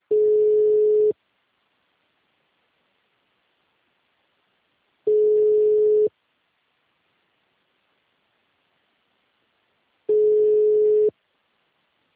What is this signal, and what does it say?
Ringback tone (Europe, 425 Hz).